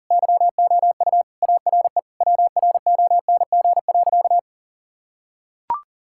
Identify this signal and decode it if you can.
Morse code, heard as a keyed tone, about 37 wpm, 700 Hz. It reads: YOU ARE WRONG.
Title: Sentences from Top 1000 Words - Review Part 5 - 40wpm